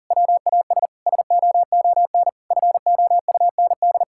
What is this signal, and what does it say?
Morse code, heard as a keyed tone, about 40 wpm, 700 Hz. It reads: WAS SOON FOUND
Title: Sentences from Top 1000 Words - Review Part 5 - 40wpm